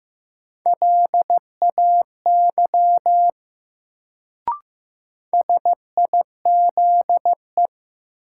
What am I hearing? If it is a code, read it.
Morse code, heard as a keyed tone, about 15 wpm, 700 Hz. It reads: LAY SIZE
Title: Sentences from Top 400 Words - New Word Drill - 15wpm